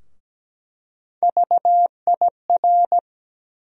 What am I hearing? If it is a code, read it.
Morse code, heard as a keyed tone, about 17 wpm, 700 Hz. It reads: VIR
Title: USA Call Sign Postfixes - 3 Letters 17wpm